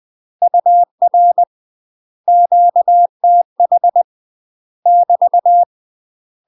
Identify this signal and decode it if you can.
Morse code, heard as a keyed tone, about 20 wpm, 700 Hz. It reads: UR QTH <BT>
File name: UK MASTER QSOs TRAINING / 20WPM / UK MASTER QSOs 0056